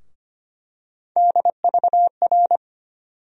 Morse code, heard as a keyed tone, about 25 wpm, 700 Hz. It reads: DVR
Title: USA Call Sign Postfixes - 3 Letters 25wpm